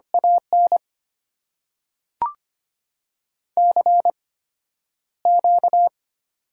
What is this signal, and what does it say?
Morse code, heard as a keyed tone, about 25 wpm, spaced as for 11 wpm, 690 Hz. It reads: AN CQ
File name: Top 90 QSO Elements - 8x Character Spacing - 25wpm